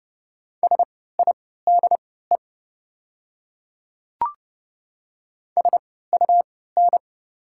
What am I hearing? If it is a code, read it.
Morse code, heard as a keyed tone, about 30 wpm, spaced as for 17 wpm, 700 Hz. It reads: SIDE SUN
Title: 3x Character Spacing - Top 300 Words - 30wpm